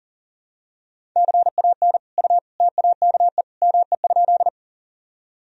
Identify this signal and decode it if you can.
Morse code, heard as a keyed tone, about 37 wpm, 700 Hz. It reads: CAN U TAKE ME?
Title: Sentences from Top 200 Words and Abbreviations 40wpm